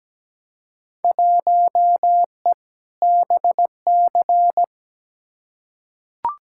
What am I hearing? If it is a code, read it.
Morse code, heard as a keyed tone, about 17 wpm, 700 Hz. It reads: 1E BC
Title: Field Day Exchanges - 17wpm